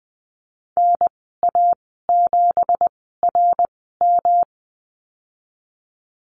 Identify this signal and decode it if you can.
Morse code, heard as a keyed tone, about 20 wpm, spaced as for 14 wpm, 700 Hz. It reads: NA7RM
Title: 238 Lesson 38g - Letter q - Call Signs 20wpm